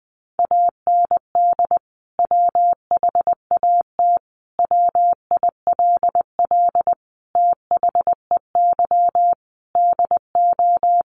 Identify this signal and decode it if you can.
Morse code, heard as a keyed tone, about 20 wpm, 700 Hz. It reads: AND WHAT WILL THEY DO
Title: Sentences from Top 200 Words 20wpm